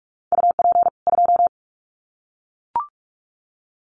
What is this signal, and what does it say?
Morse code, heard as a keyed tone, about 44 wpm, 700 Hz. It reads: UP 2
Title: Common QSO Phrases 45wpm